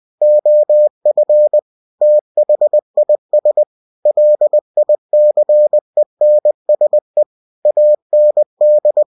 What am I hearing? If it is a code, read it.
Morse code, heard as a keyed tone, about 20 wpm, 600 Hz. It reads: OF THIS LICENSE AND
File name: cw-books / winnie-the-pooh / winnie-20wpm-page-0113